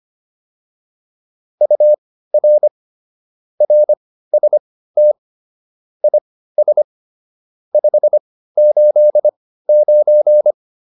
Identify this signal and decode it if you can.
Morse code, heard as a keyed tone, about 25 wpm, spaced as for 15 wpm, 600 Hz. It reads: UR RST IS 589
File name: QSO 302 15